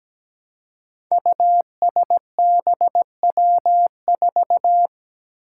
Morse code, heard as a keyed tone, about 17 wpm, 700 Hz. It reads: USBW4
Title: Five Challenging Letter-Number Combinations 17wpm